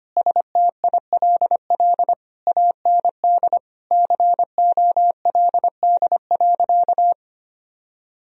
Morse code, heard as a keyed tone, about 25 wpm, 700 Hz. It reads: STILL AND COLD.
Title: Sentences from Top 1000 Words - Review Part 2 - 25wpm